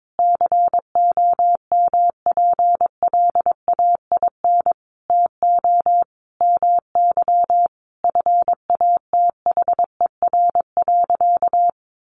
Morse code, heard as a keyed tone, about 22 wpm, 700 Hz. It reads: COMPLAIN TO MY FATHER.
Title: Sentences from Top 1300 Words 22wpm